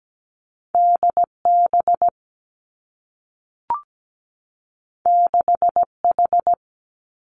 Morse code, heard as a keyed tone, about 17 wpm, 700 Hz. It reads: D B 6 H
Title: Two Challenging Letter-Number Combinations 17wpm